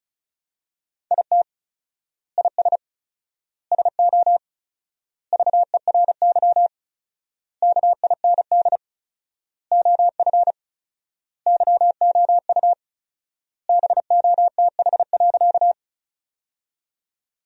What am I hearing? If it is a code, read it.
Morse code, heard as a keyed tone, about 36 wpm, 700 Hz. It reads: IT IS SO VERY KIND OF YOU BOTH.
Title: 4x Word-Spacing - Sentences from Top 300 Words - 35wpm